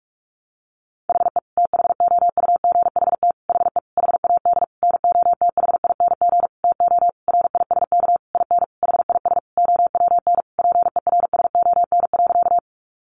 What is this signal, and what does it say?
Morse code, heard as a keyed tone, about 44 wpm, 700 Hz. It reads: HE THOUGHT HE HAD NOTHING TO RISK IN HIS OWN PERSON.
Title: Sentences from Top 600 Words 45wpm